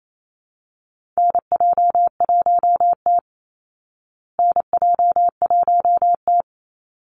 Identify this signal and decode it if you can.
Morse code, heard as a keyed tone, about 28 wpm, 690 Hz. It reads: NJ1T NJ1T
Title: Call Signs Repeated 3x - 28wpm